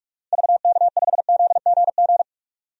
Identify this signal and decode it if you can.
Morse code, heard as a keyed tone, about 46 wpm, 700 Hz. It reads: VK5ZGG
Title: DXCC-100 Call Signs - 6 Characters - 45wpm